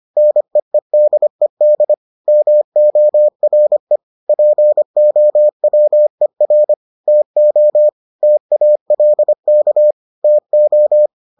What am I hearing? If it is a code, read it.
Morse code, heard as a keyed tone, about 25 wpm, 600 Hz. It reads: NEEDED MORE POWER TO TALK TO